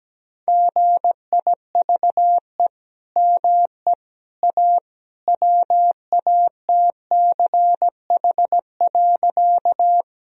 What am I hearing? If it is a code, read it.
Morse code, heard as a keyed tone, about 17 wpm, 700 Hz. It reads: GIVE ME A WATCH.
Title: Sentences from Top 1000 Words - Review Part 1 - 17wpm